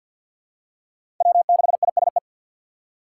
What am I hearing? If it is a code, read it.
Morse code, heard as a keyed tone, about 53 wpm, 700 Hz. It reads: W6ISE
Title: USA Call Signs - 5 Characters - 2x Character Spacing - 50wpm